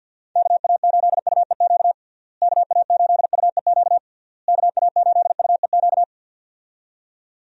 Morse code, heard as a keyed tone, about 50 wpm, 700 Hz. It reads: KA8UEQ KA8UEQ KA8UEQ
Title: USA Call Signs - 6 Characters - Repeated 3x - 50wpm